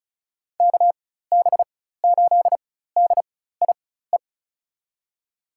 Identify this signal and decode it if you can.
Morse code, heard as a keyed tone, about 36 wpm, spaced as for 16 wpm, 700 Hz. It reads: KB8DIE
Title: USA Call Signs - 4x Character Spacing - 35wpm